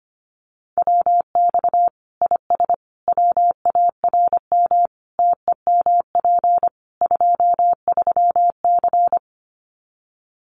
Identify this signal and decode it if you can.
Morse code, heard as a keyed tone, about 25 wpm, 700 Hz. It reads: WX IS WARM TEMP 23C
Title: Common QSO Phrases 25wpm